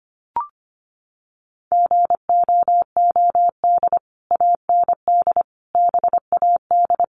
Morse code, heard as a keyed tone, about 25 wpm, 700 Hz. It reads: GOOD AND BAD
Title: Sentences from Top 1500 Words 25wpm